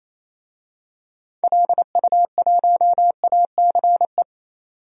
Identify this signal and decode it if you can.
Morse code, heard as a keyed tone, about 28 wpm, 700 Hz. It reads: LU1ACE
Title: DXCC-100 Call Signs - 6 Characters - 28wpm